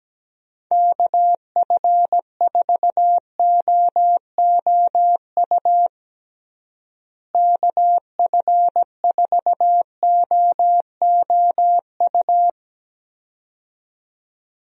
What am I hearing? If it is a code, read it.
Morse code, heard as a keyed tone, about 17 wpm, 700 Hz. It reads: KF4OOU KF4OOU
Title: USA Call Signs - 6 Characters - Repeated 2x - 17wpm